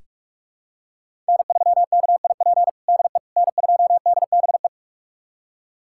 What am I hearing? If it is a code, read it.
Morse code, heard as a keyed tone, about 46 wpm, 700 Hz. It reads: N3KIP DE N2DBE
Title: USA Callsign DE Callsign - 5 Characters - 45wpm